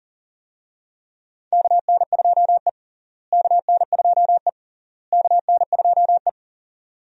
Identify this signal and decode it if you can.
Morse code, heard as a keyed tone, about 40 wpm, 700 Hz. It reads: KN2E KN2E KN2E
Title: USA Call Signs - 4 Characters - Repeated 3x - 40wpm